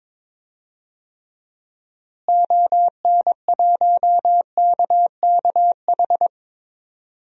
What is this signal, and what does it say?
Morse code, heard as a keyed tone, about 22 wpm, 700 Hz. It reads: ON1KKH
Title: DXCC-100 Call Signs - 6 Characters - Repeated 2x - 22wpm